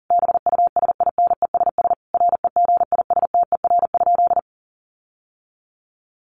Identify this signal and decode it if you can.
Morse code, heard as a keyed tone, about 40 wpm, 700 Hz. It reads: BUSINESS REGISTER?
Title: Sentences from Top 1200 Words 40wpm